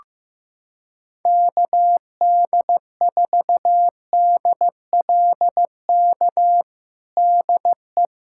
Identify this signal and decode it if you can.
Morse code, heard as a keyed tone, about 15 wpm, 700 Hz. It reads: KD4DLK DE
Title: USA Callsign DE Callsign - 6 Characters - 15wpm